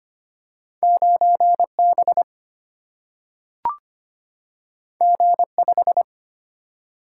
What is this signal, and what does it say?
Morse code, heard as a keyed tone, about 25 wpm, 700 Hz. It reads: 9B G5
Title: Single-Letter Single-Number 25wpm